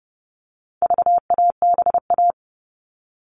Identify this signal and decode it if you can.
Morse code, heard as a keyed tone, about 30 wpm, 700 Hz. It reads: VABA
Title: All Cut Numbers - 4-Digit 30wpm